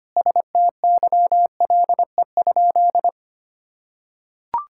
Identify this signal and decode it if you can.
Morse code, heard as a keyed tone, about 25 wpm, 700 Hz. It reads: STYLE?
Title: Sentences from Top 1000 Words - Review Part 5 - 25wpm